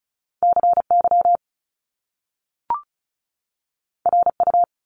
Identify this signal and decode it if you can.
Morse code, heard as a keyed tone, about 36 wpm, 700 Hz. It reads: CY RU
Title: Top Two Letter Combinations in Words 35wpm